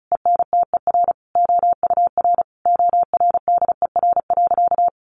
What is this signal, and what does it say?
Morse code, heard as a keyed tone, about 34 wpm, 700 Hz. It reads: ENTER OUR ORDER.
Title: Sentences from Top 1300 Words 35wpm